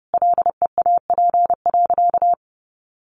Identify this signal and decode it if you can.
Morse code, heard as a keyed tone, about 30 wpm, 700 Hz. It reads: LEAP.
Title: A Sound Of Thunder 30wpm